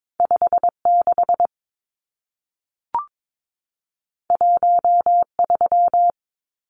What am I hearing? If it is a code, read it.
Morse code, heard as a keyed tone, about 22 wpm, 700 Hz. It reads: 56 13
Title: Numbers - 2-Digit 22wpm